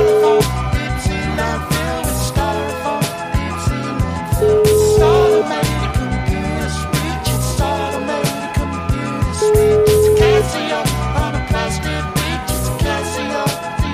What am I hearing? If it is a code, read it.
Ringback tone (Europe, 425 Hz), with noise.